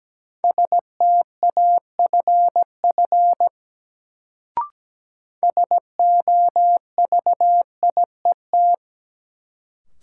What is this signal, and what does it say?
Morse code, heard as a keyed tone, about 17 wpm, 700 Hz. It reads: STAFF SOVIET
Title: Sets of 5 Words Encore - New Word Drill - 17wpm